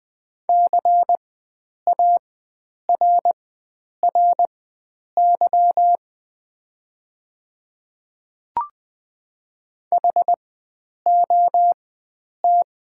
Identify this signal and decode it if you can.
Morse code, heard as a keyed tone, about 20 wpm, spaced as for 9 wpm, 700 Hz. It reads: CARRY HOT
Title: 4x Character Spacing - Top 500 Words - 20wpm